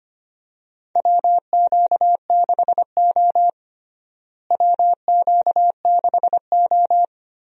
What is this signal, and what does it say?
Morse code, heard as a keyed tone, about 25 wpm, 700 Hz. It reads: WQ6O WQ6O
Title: USA Call Signs - 4 Characters - Repeated 3x - 25wpm